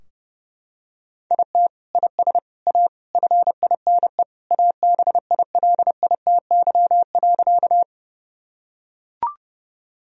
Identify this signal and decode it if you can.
Morse code, heard as a keyed tone, about 30 wpm, 700 Hz. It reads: IT IS A FINE ABILITY.
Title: Sentences from Top 1000 Words - Review Part 3 - 30wpm